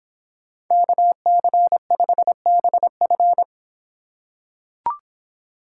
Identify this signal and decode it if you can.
Morse code, heard as a keyed tone, about 24 wpm, 700 Hz. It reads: KC5BF